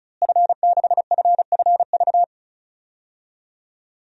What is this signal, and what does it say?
Morse code, heard as a keyed tone, about 36 wpm, 700 Hz. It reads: F6FFV
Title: DXCC-100 Call Signs - 5 Characters - Repeated 2x - 35wpm